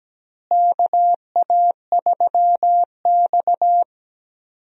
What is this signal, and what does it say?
Morse code, heard as a keyed tone, about 17 wpm, 700 Hz. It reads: KA3X
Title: USA Call Signs - 4 Characters - Repeated 3x - 17wpm